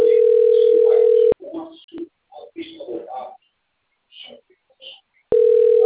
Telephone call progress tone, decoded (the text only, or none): ringback tone